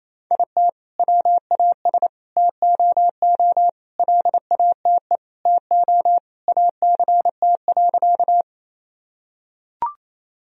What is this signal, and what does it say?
Morse code, heard as a keyed tone, about 28 wpm, 700 Hz. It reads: IT WAS TOO LATE TO ACT.
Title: Sentences from Top 1000 Words - Review Part 1 - 28wpm